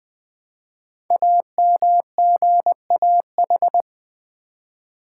Morse code, heard as a keyed tone, about 21 wpm, 700 Hz. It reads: AMGAH
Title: Code Groups - 5 Letters - 20wpm